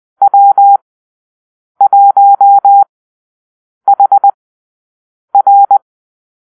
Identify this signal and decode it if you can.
Morse code, heard as a keyed tone, about 20 wpm, spaced as for 7 wpm, 800 Hz. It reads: W1HR
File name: arrl / Call-Sign Drill-8